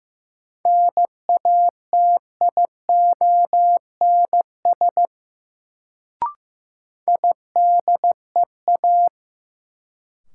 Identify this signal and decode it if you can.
Morse code, heard as a keyed tone, about 15 wpm, 700 Hz. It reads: NATIONS IDEA
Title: Sets of 3 Words - New Word Drill - 15wpm